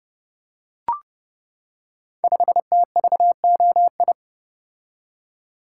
Morse code, heard as a keyed tone, about 30 wpm, 700 Hz. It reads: 5TVOI
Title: Code Groups - 5 Letter:Numbers - 30wpm